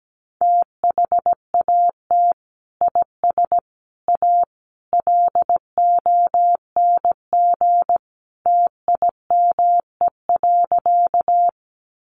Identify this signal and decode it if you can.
Morse code, heard as a keyed tone, about 17 wpm, 700 Hz. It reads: THAT IS A LONG TIME.
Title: Sentences from Top 100 Words 17wpm